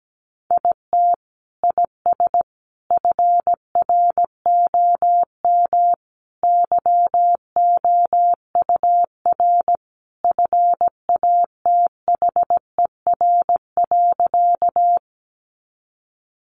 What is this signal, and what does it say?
Morse code, heard as a keyed tone, about 17 wpm, 700 Hz. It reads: IT IS FROM YOUR FATHER.
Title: Sentences from Top 300 Words 17wpm